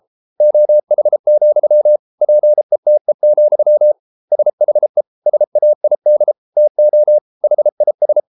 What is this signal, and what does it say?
Morse code, heard as a keyed tone, about 33 wpm, 600 Hz. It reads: OH, PETE, SHE SAID TO HIS